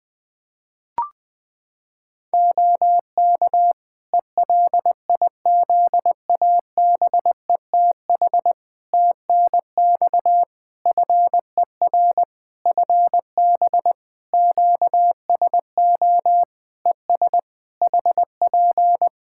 Morse code, heard as a keyed tone, about 21 wpm, 700 Hz. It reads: OK ELIZABETH TNX FER FB QSO ES HP
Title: Practice QSO - Exchange 4 - Ending - 20wpm